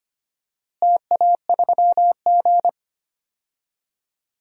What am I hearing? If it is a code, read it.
Morse code, heard as a keyed tone, about 25 wpm, 700 Hz. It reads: TA3G
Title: DXCC-100 Call Signs - 4 Characters - Repeated 3x - 25wpm